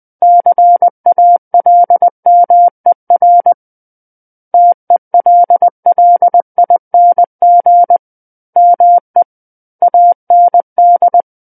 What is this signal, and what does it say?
Morse code, heard as a keyed tone, about 20 wpm, 700 Hz. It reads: CALMER TELLING ME AND
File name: 20 WPM